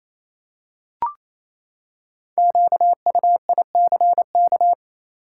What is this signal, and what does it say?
Morse code, heard as a keyed tone, about 28 wpm, 700 Hz. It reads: QUICK
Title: Five Letter Words - 28wpm